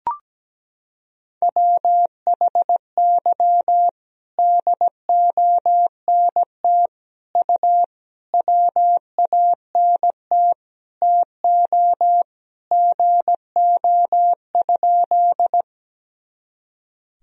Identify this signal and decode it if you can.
Morse code, heard as a keyed tone, about 17 wpm, 700 Hz. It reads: WHY DONT U WANT TO GO?
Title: Sentences from Top 300 Words and Abbreviations 17wpm